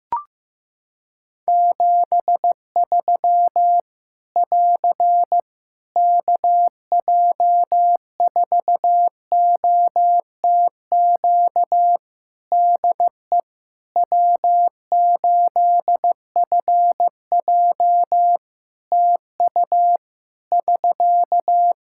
Morse code, heard as a keyed tone, about 15 wpm, 700 Hz. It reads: 73 <AR> KJ4OTQ DE W8FJ TU <SK>
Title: Practice QSO - Exchange 4 - Ending - 15wpm